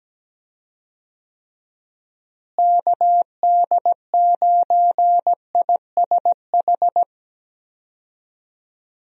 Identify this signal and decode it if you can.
Morse code, heard as a keyed tone, about 17 wpm, 700 Hz. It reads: KD9ISH
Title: USA Call Signs - 6 Characters - Repeated 3x - 17wpm